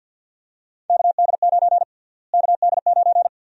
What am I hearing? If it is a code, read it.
Morse code, heard as a keyed tone, about 50 wpm, 700 Hz. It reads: KD9 KD9
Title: USA Call Sign Prefixes - Repeated 2x - 50wpm